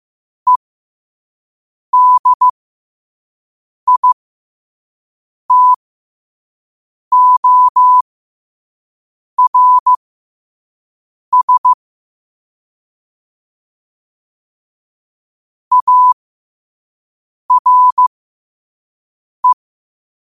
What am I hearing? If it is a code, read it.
Morse code, heard as a keyed tone, about 15 wpm, spaced as for 5 wpm, 1000 Hz. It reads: EDITORS ARE
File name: qotdfiles / 5wpm